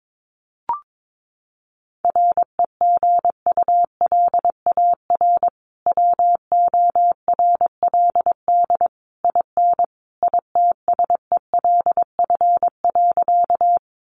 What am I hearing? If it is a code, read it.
Morse code, heard as a keyed tone, about 22 wpm, 700 Hz. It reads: REGULAR WORLD IN ITSELF.
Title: Sentences from Top 900 Words 22wpm